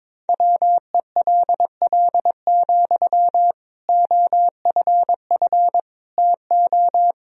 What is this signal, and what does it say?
Morse code, heard as a keyed tone, about 22 wpm, 700 Hz. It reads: WELL, OFF TO
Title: Sentences from Top 1000 Words - Review Part 3 - 22wpm